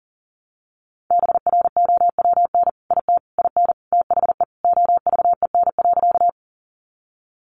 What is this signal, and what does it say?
Morse code, heard as a keyed tone, about 40 wpm, 700 Hz. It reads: BROWN IT IN THE OVEN.
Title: Sentences from Top 1200 Words 40wpm